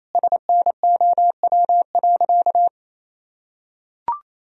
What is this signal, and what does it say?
Morse code, heard as a keyed tone, about 27 wpm, 700 Hz. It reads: SNOW.
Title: Sentences from Top 1000 Words - Review Part 3 - 28wpm